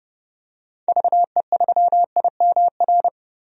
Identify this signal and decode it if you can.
Morse code, heard as a keyed tone, about 32 wpm, 700 Hz. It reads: VE3IMR
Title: DXCC-100 Call Signs - 6 Characters - 30wpm